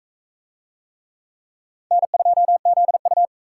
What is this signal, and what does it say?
Morse code, heard as a keyed tone, about 44 wpm, 700 Hz. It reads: N2ZU